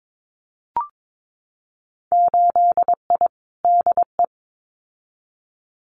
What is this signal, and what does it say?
Morse code, heard as a keyed tone, about 22 wpm, 700 Hz. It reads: 8I DE
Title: Winter Field Day Exchanges Encore 22wpm